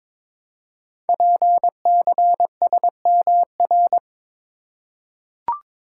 Morse code, heard as a keyed tone, about 23 wpm, 700 Hz. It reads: PCSMR
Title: Code Groups - 5 Letter:Numbers - 22wpm